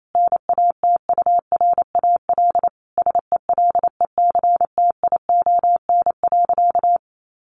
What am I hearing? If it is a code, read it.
Morse code, heard as a keyed tone, about 28 wpm, 700 Hz. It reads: NATURAL SELECTION.
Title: Sentences from Top 1400 Words 28wpm